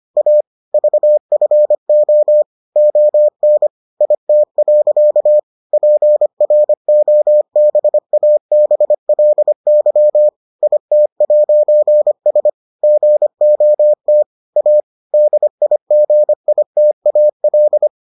Morse code, heard as a keyed tone, about 25 wpm, 600 Hz. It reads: A VFO ON IT. PROBABLY IT'S GOT A DIGITAL